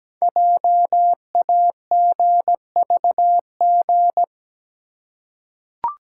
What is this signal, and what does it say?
Morse code, heard as a keyed tone, about 17 wpm, 700 Hz. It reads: JAGVG
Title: Code Groups - 5 Letters - 17wpm